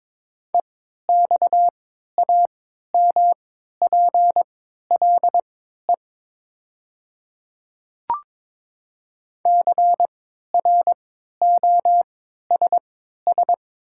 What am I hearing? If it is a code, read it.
Morse code, heard as a keyed tone, about 22 wpm, spaced as for 13 wpm, 700 Hz. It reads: EXAMPLE CROSS
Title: 3x Character Spacing - Top 500 Words - 22wpm